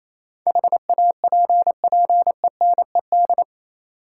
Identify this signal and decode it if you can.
Morse code, heard as a keyed tone, about 28 wpm, 700 Hz. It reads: HAPPENED
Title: Sentences from Top 1000 Words - Review Part 2 - New Word Drill - 28wpm